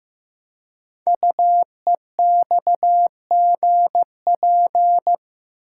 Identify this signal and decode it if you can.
Morse code, heard as a keyed tone, about 15 wpm, 700 Hz. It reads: UEXGP
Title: Code Groups - 5 Letters - 15wpm